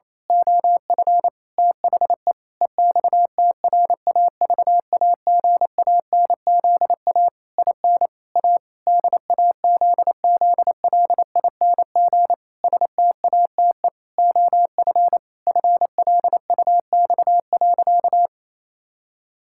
Morse code, heard as a keyed tone, about 28 wpm, 700 Hz. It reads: OF THE EXTRAVAGANZA IN A DAZZLING STATE OF FLUX.
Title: Pangram Challenge v2 - 28wpm